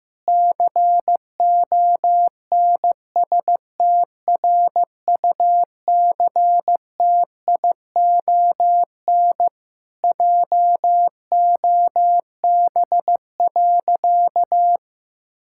Morse code, heard as a keyed tone, about 15 wpm, 700 Hz. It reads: CONSTRUCTION JOB.
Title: Sentences from Top 1600 Words 15wpm